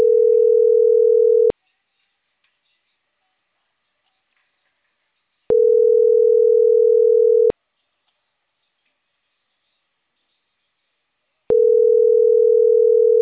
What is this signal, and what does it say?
Ringback tone (North America), 440 + 480 Hz.